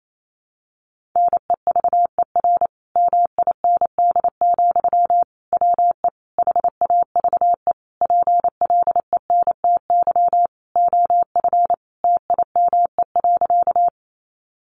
Morse code, heard as a keyed tone, about 28 wpm, 700 Hz. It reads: NEVER MIND, WE HAVE PLENTY OF TIME.
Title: Sentences from Top 1300 Words 28wpm